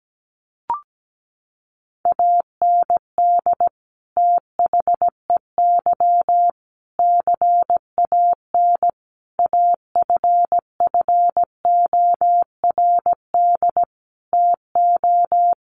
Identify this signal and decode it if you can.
Morse code, heard as a keyed tone, about 17 wpm, 700 Hz. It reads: AND THEY CAN AFFORD TO
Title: Sentences from Top 1100 Words 17wpm